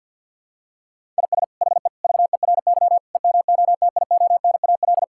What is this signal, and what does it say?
Morse code, heard as a keyed tone, about 50 wpm, 700 Hz. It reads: IS HE VERY EMOTIONAL
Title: Sentences from Top 1000 Words - Review Part 5 - 50wpm